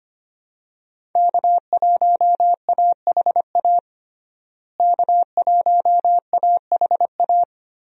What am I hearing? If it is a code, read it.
Morse code, heard as a keyed tone, about 25 wpm, 700 Hz. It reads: K1AHA K1AHA
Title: USA Call Signs - 5 Characters - Repeated 3x - 25wpm